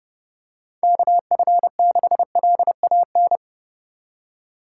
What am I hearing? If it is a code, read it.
Morse code, heard as a keyed tone, about 30 wpm, 700 Hz. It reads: KF6LAN
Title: USA Call Signs - 6 Characters - 3x Character Spacing - 30wpm